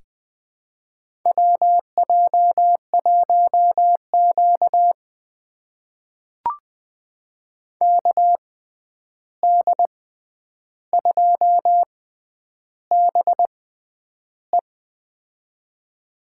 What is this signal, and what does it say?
Morse code, heard as a keyed tone, about 20 wpm, spaced as for 10 wpm, 700 Hz. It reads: WJ1Q KD2BE
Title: USA Call Signs - 6x Character Spacing - 20wpm